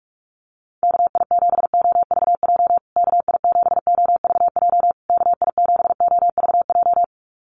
Morse code, heard as a keyed tone, about 44 wpm, 700 Hz. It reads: KI7OVJ KI7OVJ KI7OVJ
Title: USA Call Signs Repeated 3x 45wpm